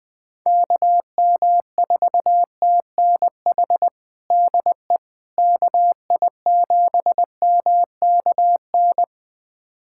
Morse code, heard as a keyed tone, about 20 wpm, 700 Hz. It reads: KM4TNH DE KI7MKN
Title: USA Callsign DE Callsign - 6 Characters - 20wpm